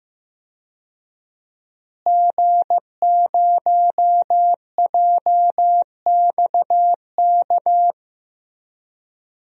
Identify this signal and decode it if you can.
Morse code, heard as a keyed tone, about 15 wpm, 700 Hz. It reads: G0JXK
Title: DXCC-100 Call Signs - 5 Characters - Repeated 2x - 15wpm